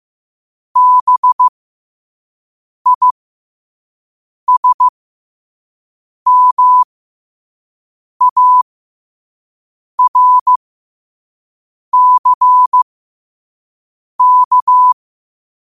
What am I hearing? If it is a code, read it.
Morse code, heard as a keyed tone, about 15 wpm, spaced as for 5 wpm, 1000 Hz. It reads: BISMARCK
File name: qotdfiles / 5wpm